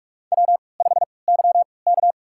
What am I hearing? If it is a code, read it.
Morse code, heard as a keyed tone, about 44 wpm, spaced as for 25 wpm, 700 Hz. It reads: W5YK
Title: USA Call Signs - 4 Characters - 3x Character Spacing - 45wpm